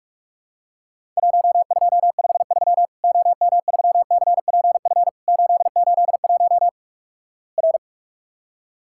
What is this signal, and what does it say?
Morse code, heard as a keyed tone, about 47 wpm, 700 Hz. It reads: 1253 OM3CPF 881 R
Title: Worked All Europe DX Contest - QTC Exchange - 45wpm